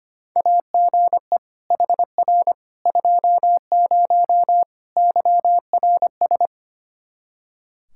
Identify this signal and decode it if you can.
Morse code, heard as a keyed tone, about 25 wpm, 700 Hz. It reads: AGE HR 20 YRS
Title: Practice QSO - Exchange 3 - More Info - 25wpm